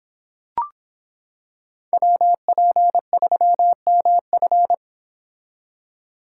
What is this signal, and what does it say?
Morse code, heard as a keyed tone, about 26 wpm, 700 Hz. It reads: WP3MF